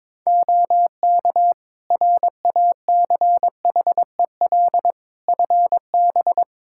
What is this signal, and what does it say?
Morse code, heard as a keyed tone, about 21 wpm, 700 Hz. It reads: OK RACHEL FB
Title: K1USN SST QSO Practice 22wpm